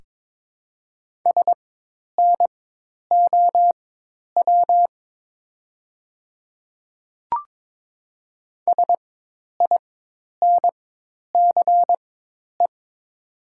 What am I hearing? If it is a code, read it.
Morse code, heard as a keyed tone, about 22 wpm, spaced as for 10 wpm, 700 Hz. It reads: SNOW SINCE
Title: 4x Character Spacing - Top 500 Words - 22wpm